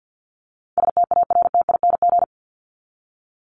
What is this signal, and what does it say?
Morse code, heard as a keyed tone, about 48 wpm, 700 Hz. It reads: STARTING
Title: ING Suffix - Top 100 Words with Base Words 50wpm